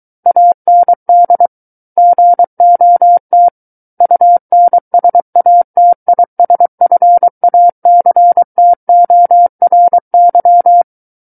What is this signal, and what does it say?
Morse code, heard as a keyed tone, about 23 wpm, 700 Hz. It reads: AND GOT UNSATISFACTORY
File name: Book One / 14-Chapter 14 - 23 WPM